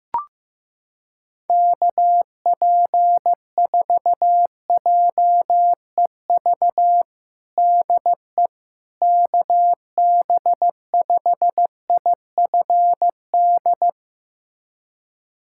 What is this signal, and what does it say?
Morse code, heard as a keyed tone, about 15 wpm, 700 Hz. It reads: KP4JEV DE KB5IFD
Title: USA Callsign DE Callsign 15wpm